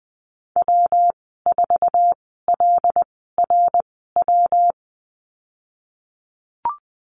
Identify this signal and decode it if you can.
Morse code, heard as a keyed tone, about 20 wpm, spaced as for 15 wpm, 700 Hz. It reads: W4LRW
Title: 219 Lesson 35g - Number 0 - Call Signs 20wpm